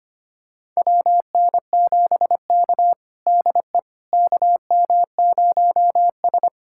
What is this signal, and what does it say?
Morse code, heard as a keyed tone, about 25 wpm, 700 Hz. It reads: WN7K DE KM0S
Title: USA Callsign DE Callsign - 4 Characters - 25wpm